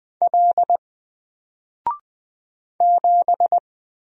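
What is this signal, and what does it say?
Morse code, heard as a keyed tone, about 20 wpm, 700 Hz. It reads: L 7
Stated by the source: Single letter and number Morse drill at 20 wpm, from N4LCD.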